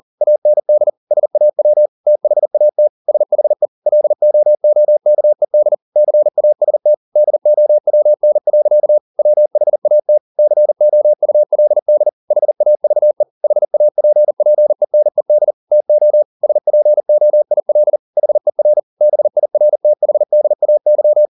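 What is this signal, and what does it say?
Morse code, heard as a keyed tone, about 40 wpm, 600 Hz. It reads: AND SAW THAT SHE LOOKED CAST DOWN. WHAT COULD HAVE HAPPENED TO SPOIL HER BIRTHDAY
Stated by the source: Morse code at 40 wpm, practice text, CW med Gnister.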